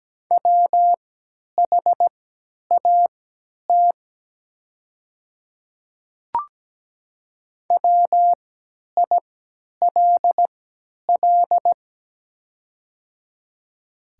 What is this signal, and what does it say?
Morse code, heard as a keyed tone, about 17 wpm, spaced as for 10 wpm, 700 Hz. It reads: WHAT WILL
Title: 3x Character Spacing - Top 100 Words - 17wpm